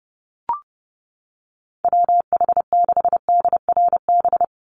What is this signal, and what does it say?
Morse code, heard as a keyed tone, about 30 wpm, 700 Hz. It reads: WH6DRB
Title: USA Call Signs 30wpm